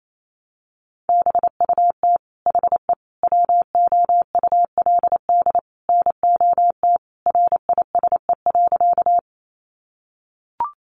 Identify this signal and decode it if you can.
Morse code, heard as a keyed tone, about 28 wpm, 700 Hz. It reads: BUT HE WOULD NOT RISE.
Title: Sentences from Top 900 Words 28wpm